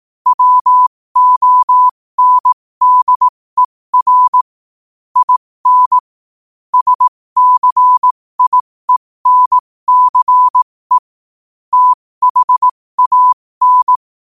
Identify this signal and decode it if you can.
Morse code, heard as a keyed tone, about 18 wpm, spaced as for 15 wpm, 1000 Hz. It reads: WONDER IN SCIENCE THAN
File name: qotdfiles / 15wpm